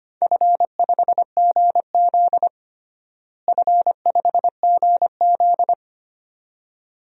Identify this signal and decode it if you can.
Morse code, heard as a keyed tone, about 25 wpm, 700 Hz. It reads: F5GZ F5GZ
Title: DXCC-100 Call Signs - 4 Characters - Repeated 2x - 25wpm